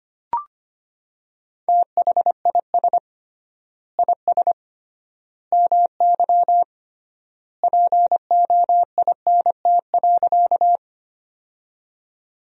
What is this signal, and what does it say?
Morse code, heard as a keyed tone, about 25 wpm, 700 Hz. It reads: THIS IS MY POINT.
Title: 3x Word-Spacing - Sentences from Top 200 Words - 25wpm